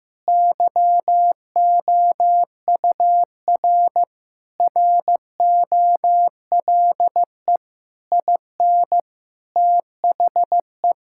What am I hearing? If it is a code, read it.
Morse code, heard as a keyed tone, about 16 wpm, 700 Hz. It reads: YOUR ROLE IN THE
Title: Sentences from Top 1000 Words - Review Part 6 - 15wpm